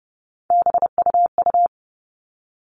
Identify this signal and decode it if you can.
Morse code, heard as a keyed tone, about 30 wpm, 700 Hz. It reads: BUU
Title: All Cut Numbers - 3-Digit 30wpm